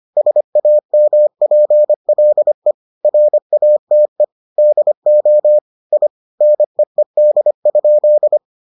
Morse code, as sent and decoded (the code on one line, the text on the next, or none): ... .- -- .--. .-.. / .-. .- - . / -.. --- / .. / -. . -.. ..--..
SAMPLE RATE DO I NEED?